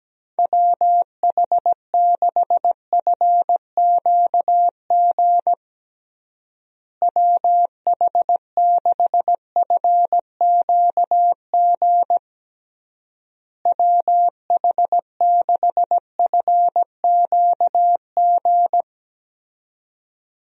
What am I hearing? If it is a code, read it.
Morse code, heard as a keyed tone, about 17 wpm, 700 Hz. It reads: WH6FQG WH6FQG WH6FQG
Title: USA Call Signs - 6 Characters - Repeated 3x - 17wpm